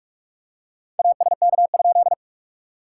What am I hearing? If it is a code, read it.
Morse code, heard as a keyed tone, about 44 wpm, 700 Hz. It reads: ASK?
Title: SOTA QSO Practice 45wpm